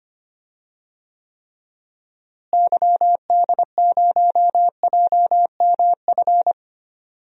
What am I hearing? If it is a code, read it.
Morse code, heard as a keyed tone, about 25 wpm, 700 Hz. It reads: YD0JMF
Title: DXCC-100 Call Signs - 6 Characters - Repeated 3x - 25wpm